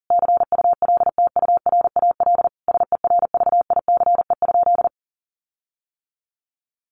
Morse code, heard as a keyed tone, about 40 wpm, 700 Hz. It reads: CULTURAL SERVICE?
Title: Sentences from Top 1500 Words 40wpm